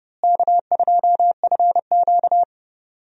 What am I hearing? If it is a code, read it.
Morse code, heard as a keyed tone, about 30 wpm, 700 Hz. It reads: K2FQ
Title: USA Call Signs - 4 Characters - Repeated 3x - 30wpm